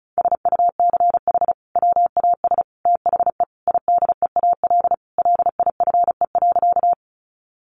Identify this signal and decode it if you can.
Morse code, heard as a keyed tone, about 34 wpm, 700 Hz. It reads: SUCH WAS THE IDEAL LIFE.
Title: Sentences from Top 1400 Words 35wpm